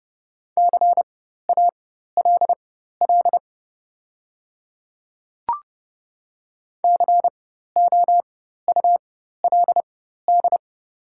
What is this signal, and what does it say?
Morse code, heard as a keyed tone, about 30 wpm, spaced as for 14 wpm, 700 Hz. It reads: CALL COULD
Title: 4x Character Spacing - Top 100 Words - 30wpm